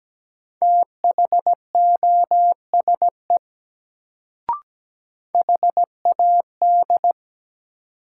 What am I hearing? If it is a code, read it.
Morse code, heard as a keyed tone, about 17 wpm, 700 Hz. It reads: THOSE HAD
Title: Sets of 2 Words - New Word Drill - 17wpm